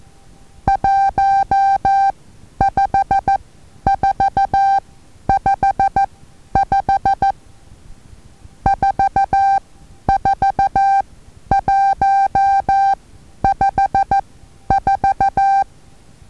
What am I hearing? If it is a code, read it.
Morse code, heard as a keyed tone, about 14 wpm, spaced as for 10 wpm, 790 Hz. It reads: 15455 44154
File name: Taviro HA6VA A